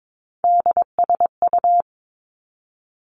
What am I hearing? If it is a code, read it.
Morse code, heard as a keyed tone, about 22 wpm, 700 Hz. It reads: DSU
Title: Three Challenging Letter-Number Combinations 22wpm